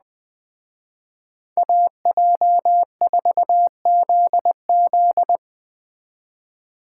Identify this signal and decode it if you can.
Morse code, heard as a keyed tone, about 20 wpm, 700 Hz. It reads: AJ4ZZ
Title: Speed Racing - USA 2x2 Call Signs 17wpm